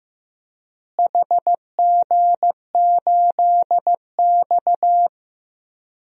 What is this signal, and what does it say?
Morse code, heard as a keyed tone, about 16 wpm, 700 Hz. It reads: HG8X
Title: DXCC-100 Call Signs - 4 Characters - Repeated 3x - 15wpm